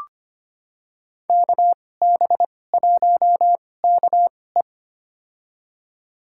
Morse code, heard as a keyed tone, about 25 wpm, spaced as for 18 wpm, 700 Hz. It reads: KB1KE
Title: USA Call Signs - 5 Characters - 2x Character Spacing - 25wpm